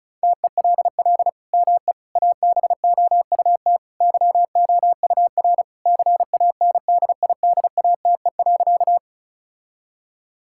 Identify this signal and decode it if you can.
Morse code, heard as a keyed tone, about 36 wpm, 700 Hz. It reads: TELL ME ABOUT YOUR CANDIDATE.
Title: Sentences from Top 1600 Words 35wpm